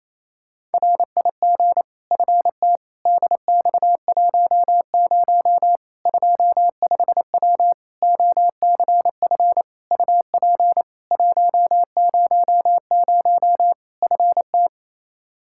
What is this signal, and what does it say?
Morse code, heard as a keyed tone, about 28 wpm, 700 Hz. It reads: RIG FT DX10 25W OCF UP 100 FT
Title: Practice QSO - Exchange 2 - Info - 28wpm